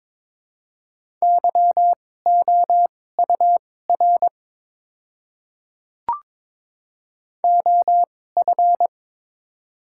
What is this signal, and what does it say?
Morse code, heard as a keyed tone, about 22 wpm, spaced as for 16 wpm, 700 Hz. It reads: YOUR OF
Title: 2x Character Spacing - Top 100 Words - 22wpm